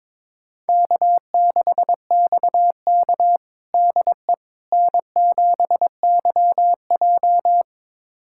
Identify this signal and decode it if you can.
Morse code, heard as a keyed tone, about 22 wpm, 700 Hz. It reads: K6XK DE N7YJ
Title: USA Callsign DE Callsign - 4 Characters - 22wpm